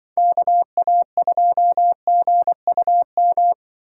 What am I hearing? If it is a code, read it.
Morse code, heard as a keyed tone, about 24 wpm, 700 Hz. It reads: KA2GUM
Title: Speed Racing - USA 2x3 Call Signs 20wpm